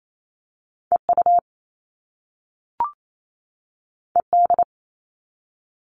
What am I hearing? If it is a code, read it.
Morse code, heard as a keyed tone, about 28 wpm, 700 Hz. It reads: EU ED
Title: All Cut Numbers - 2-Digit 28wpm